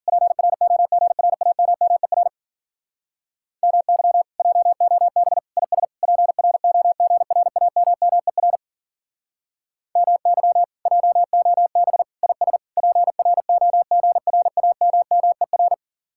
Morse code, heard as a keyed tone, about 48 wpm, 700 Hz. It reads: PROGRAMMER MY JOB IS PROGRAMMER MY JOB IS PROGRAMMER
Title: Speed-Racing Common QSO Phrases Encore - 40wpm